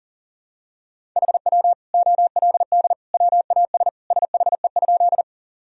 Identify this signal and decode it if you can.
Morse code, heard as a keyed tone, about 37 wpm, 700 Hz. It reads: HW OLD WAS SHE?
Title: Sentences from Top 200 Words and Abbreviations 40wpm